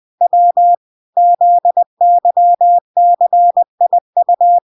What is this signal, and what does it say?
Morse code, heard as a keyed tone, about 20 wpm, 700 Hz. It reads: W ZYCIU